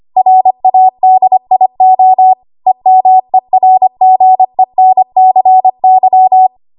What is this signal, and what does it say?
Morse code, heard as a keyed tone, about 25 wpm, 750 Hz. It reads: RADIO EMERGENCY
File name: Archive / 25 WPM / 25WPM